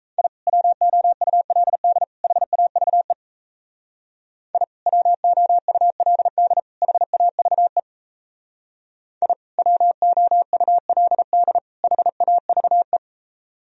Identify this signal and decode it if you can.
Morse code, heard as a keyed tone, about 36 wpm, 700 Hz. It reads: I WOULD HAVE I WOULD HAVE I WOULD HAVE